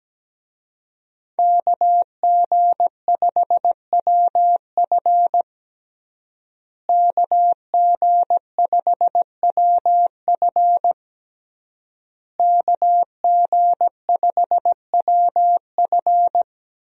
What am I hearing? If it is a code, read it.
Morse code, heard as a keyed tone, about 17 wpm, 700 Hz. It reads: KG5WF KG5WF KG5WF
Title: USA Call Signs - 5 Characters - Repeated 3x - 17wpm